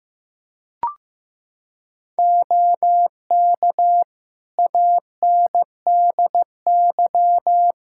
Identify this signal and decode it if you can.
Morse code, heard as a keyed tone, about 15 wpm, 700 Hz. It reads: OK ANDY
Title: K1USN SST QSO Practice 15wpm